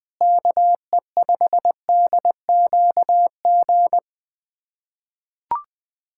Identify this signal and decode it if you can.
Morse code, heard as a keyed tone, about 20 wpm, 700 Hz. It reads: KE5DQG
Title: USA Callsign DE Callsign 20wpm